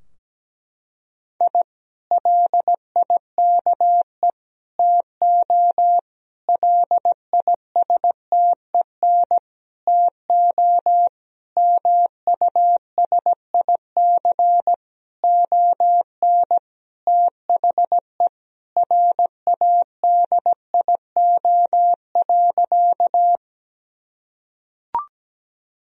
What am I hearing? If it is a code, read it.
Morse code, heard as a keyed tone, about 17 wpm, 700 Hz. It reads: I LIKE TO LISTEN TO MUSIC ON THE RADIO.
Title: Sentences from Top 1000 Words - Review Part 3 - 17wpm